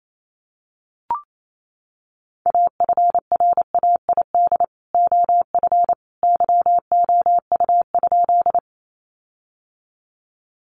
Morse code, heard as a keyed tone, about 28 wpm, 700 Hz. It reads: AFRAID OF YOU?
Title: Sentences from Top 1500 Words 28wpm